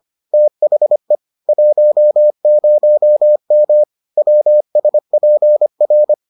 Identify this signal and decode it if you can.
Morse code, heard as a keyed tone, about 25 wpm, 600 Hz. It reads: THE 10M WSPR